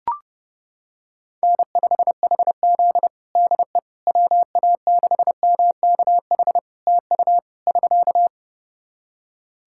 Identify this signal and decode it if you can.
Morse code, heard as a keyed tone, about 29 wpm, 700 Hz. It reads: N5HZ DE WA6MKH TU <SK>
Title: Practice QSO - Exchange 4 - Ending - 30wpm